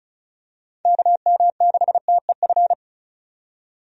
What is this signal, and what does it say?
Morse code, heard as a keyed tone, about 36 wpm, 700 Hz. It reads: KM6TEF
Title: USA Call Signs - 6 Characters - 35wpm